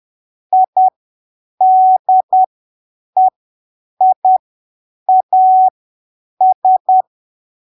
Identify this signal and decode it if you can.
Morse code, heard as a keyed tone, about 10 wpm, spaced as for 7 wpm, 750 Hz. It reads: IDEIAS
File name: arquivos / texto / texto 18 10ppm